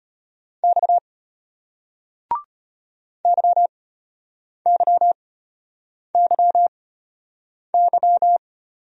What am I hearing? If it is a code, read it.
Morse code, heard as a keyed tone, about 35 wpm, spaced as for 8 wpm, 700 Hz. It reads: X YYYY